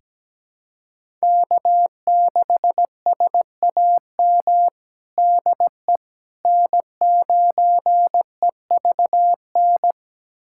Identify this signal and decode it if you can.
Morse code, heard as a keyed tone, about 17 wpm, 700 Hz. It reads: K6SAM DE N9EVN
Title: USA Callsign DE Callsign - 5 Characters - 17wpm